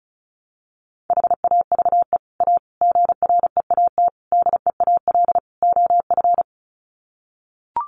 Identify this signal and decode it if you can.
Morse code, heard as a keyed tone, about 34 wpm, 700 Hz. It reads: HAVE A GREAT DEAL OF
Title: Sets of 5 Words Encore2 35wpm